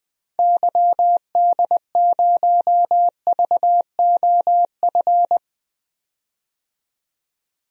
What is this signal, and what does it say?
Morse code, heard as a keyed tone, about 20 wpm, 700 Hz. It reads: YD0VOF
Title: DXCC-100 Call Signs - 6 Characters - 20wpm